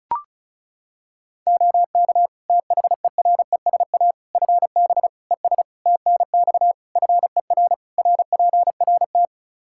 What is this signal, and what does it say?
Morse code, heard as a keyed tone, about 34 wpm, 700 Hz. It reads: OK THERESA FB ES TNX FER RPRT
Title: Practice QSO - Exchange 2 - Info - 35wpm